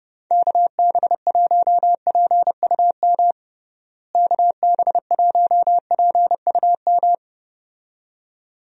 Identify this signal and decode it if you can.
Morse code, heard as a keyed tone, about 32 wpm, 700 Hz. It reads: KB1PUM KB1PUM
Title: USA Call Signs - 6 Characters - Repeated 2x - 30wpm